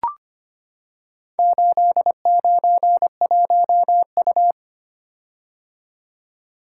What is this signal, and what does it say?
Morse code, heard as a keyed tone, about 25 wpm, 700 Hz. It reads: 891U
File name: practice / 25wpm / Four Letter-Number Combinations - 25wpm